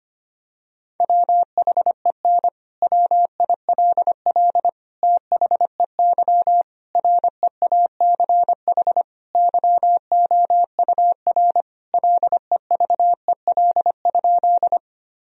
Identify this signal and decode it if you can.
Morse code, heard as a keyed tone, about 25 wpm, 700 Hz. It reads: WHEN WILL THEY REACH YOUR LEVEL?
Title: Sentences from Top 1000 Words - Review Part 4 - 25wpm